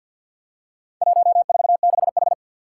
Morse code, heard as a keyed tone, about 51 wpm, 700 Hz. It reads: 146H
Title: Four Challenging Letter-Number Combinations 50wpm